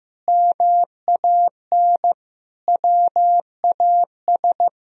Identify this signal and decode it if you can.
Morse code, heard as a keyed tone, about 15 wpm, 700 Hz. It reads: MAN WAS
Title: Sentences from Top 1000 Words - Review Part 3 - 15wpm